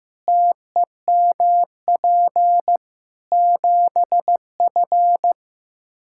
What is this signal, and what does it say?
Morse code, heard as a keyed tone, about 16 wpm, 700 Hz. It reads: TEMP 7F
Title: Practice QSO - Exchange 2 - Info - 15wpm